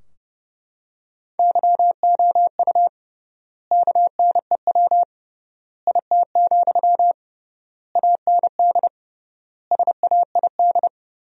Morse code, heard as a keyed tone, about 32 wpm, 700 Hz. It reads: YOU KNEW IT, AND SAID
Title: 3x Word-Spacing - Sentences from Top 500 Words - 30wpm